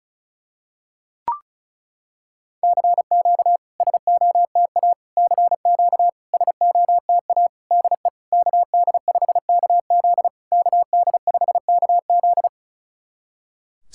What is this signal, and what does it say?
Morse code, heard as a keyed tone, about 36 wpm, 690 Hz. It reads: CQ SOTA CQ SOTA DE KD5KZ KD5KZ
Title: SOTA QSO Practice 35wpm